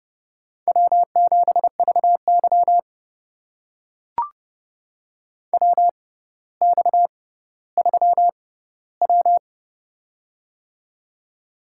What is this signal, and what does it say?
Morse code, heard as a keyed tone, about 32 wpm, spaced as for 16 wpm, 700 Hz. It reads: W7VY WX3W
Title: USA Call Signs - 4 Characters - 6x Character Spacing - 30wpm